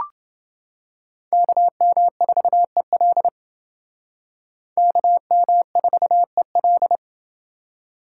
Morse code, heard as a keyed tone, about 28 wpm, 700 Hz. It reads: KM4EL KM4EL